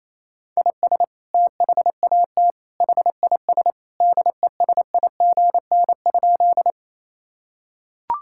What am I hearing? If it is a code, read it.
Morse code, heard as a keyed tone, about 28 wpm, 700 Hz. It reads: IS THAT HIS DESIGN?
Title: Sentences from Top 1000 Words - Review Part 5 - 28wpm